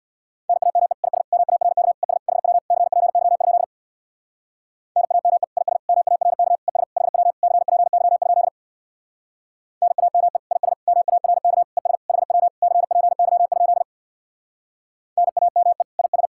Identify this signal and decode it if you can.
Morse code, heard as a keyed tone, about 66 wpm, 700 Hz. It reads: NAME IS MARK ES HW CPY? NAME IS MARK ES HW CPY? NAME IS MARK ES HW CPY? NAME IS